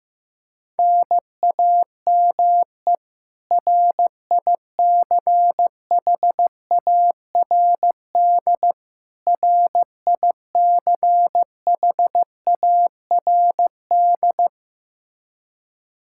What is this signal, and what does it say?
Morse code, heard as a keyed tone, about 16 wpm, 700 Hz. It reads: NAME RICHARD RICHARD
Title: Practice QSO - Exchange 1 - Beginning - 15wpm